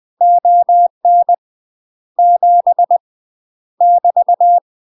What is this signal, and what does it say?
Morse code, heard as a keyed tone, about 20 wpm, 700 Hz. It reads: ON 7 <BT>
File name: UK MASTER QSOs TRAINING / 20WPM / UK MASTER QSOs 0014